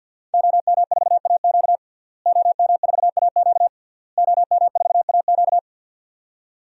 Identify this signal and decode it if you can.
Morse code, heard as a keyed tone, about 49 wpm, 700 Hz. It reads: OM4AQ OM4AQ OM4AQ
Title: DXCC-100 Call Signs - 5 Characters - Repeated 3x - 50wpm